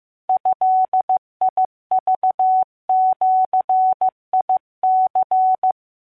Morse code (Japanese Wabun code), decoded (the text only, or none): ドクジニ